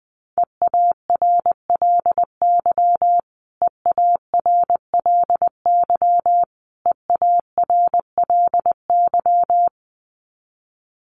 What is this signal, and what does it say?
Morse code, heard as a keyed tone, about 20 wpm, 700 Hz. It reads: EARLY EARLY EARLY
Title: Top 500 Words Repeated 20wpm